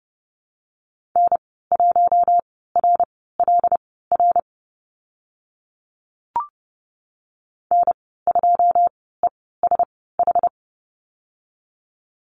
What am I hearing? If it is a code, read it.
Morse code, heard as a keyed tone, about 30 wpm, spaced as for 17 wpm, 700 Hz. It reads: N1RLR N2ESH
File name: Lesson 14g - Number 2 - Call Signs 30wpm